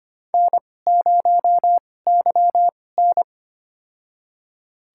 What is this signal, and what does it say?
Morse code, heard as a keyed tone, about 25 wpm, spaced as for 18 wpm, 700 Hz. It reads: N0YN
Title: USA Call Signs - 4 Characters - 2x Character Spacing - 25wpm